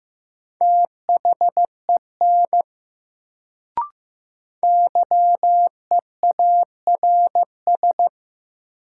Morse code, heard as a keyed tone, about 15 wpm, 700 Hz. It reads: THEN YEARS
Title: Sets of 2 Words - New Word Drill - 15wpm